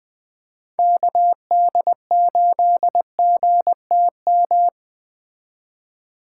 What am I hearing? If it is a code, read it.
Morse code, heard as a keyed tone, about 20 wpm, 700 Hz. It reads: KD8GTM